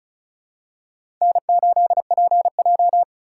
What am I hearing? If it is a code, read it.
Morse code, heard as a keyed tone, about 34 wpm, 700 Hz. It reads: N8PJ
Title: USA Call Signs - 4 Characters - 3x Character Spacing - 35wpm